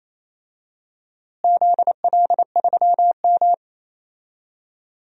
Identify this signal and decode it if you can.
Morse code, heard as a keyed tone, about 28 wpm, 700 Hz. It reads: ZL3M
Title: DXCC-100 Call Signs - 4 Characters - Repeated 2x - 28wpm